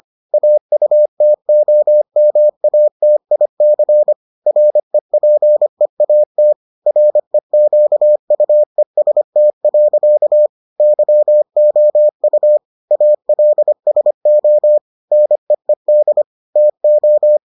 Morse code, heard as a keyed tone, about 25 wpm, 600 Hz. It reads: AUTOMATIC REPEAT REQUEST. YOU ALSO NEED TO